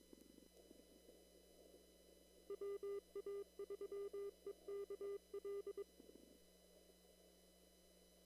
Morse code, heard as a keyed tone, about 22 wpm, 400 Hz. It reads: WA3EKL
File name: CW / kt5j wpx cw-op-k5pi